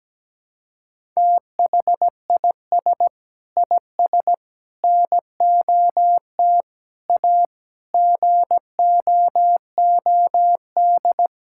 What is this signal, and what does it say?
Morse code, heard as a keyed tone, about 17 wpm, 700 Hz. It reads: THIS IS NOT A GOOD
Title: Sentences from Top 1000 Words - Review Part 6 - 17wpm